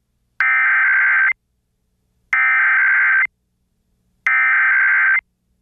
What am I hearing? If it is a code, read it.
Emergency Alert System (SAME) tones (header bursts), heard as harsh buzzing bursts. It is heard from 0.4 to 5.2 s.